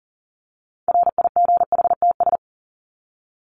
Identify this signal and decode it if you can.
Morse code, heard as a keyed tone, about 40 wpm, 700 Hz. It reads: RIGHTS
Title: ES:S Suffix - Top 100 Words with Base Words 40wpm